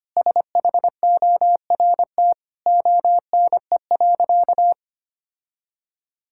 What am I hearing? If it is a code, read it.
Morse code, heard as a keyed tone, about 25 wpm, 700 Hz. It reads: SHORT ONE.
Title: Sentences from Top 1600 Words 25wpm